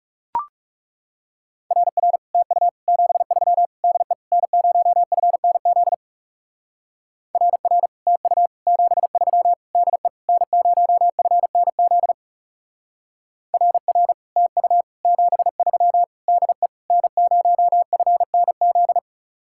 Morse code, heard as a keyed tone, about 34 wpm, 700 Hz. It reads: RR TU 73 DE N0FNZ RR TU 73 DE N0FNZ RR TU 73 DE N0FNZ
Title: Speed-Racing Common QSO Phrases - 30wpm